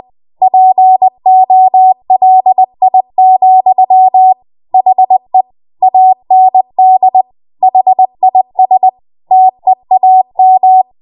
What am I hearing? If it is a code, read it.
Morse code, heard as a keyed tone, about 20 wpm, 750 Hz. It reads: POLI, HE AND HIS TEAM
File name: Morse / 20WPM